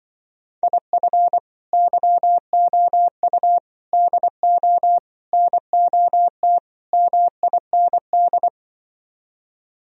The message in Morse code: IF YOU DO NOT MIND

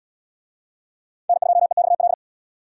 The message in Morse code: N2EXL

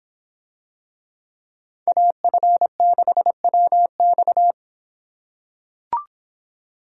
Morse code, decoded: AF6WX

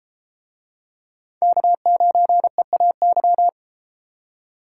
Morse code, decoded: K9EAY

